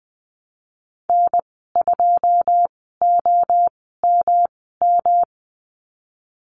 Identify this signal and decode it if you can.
Morse code, heard as a keyed tone, about 19 wpm, spaced as for 14 wpm, 700 Hz. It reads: N2OMM
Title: 122 Lesson 20g - Number 3 - Call Signs 20wpm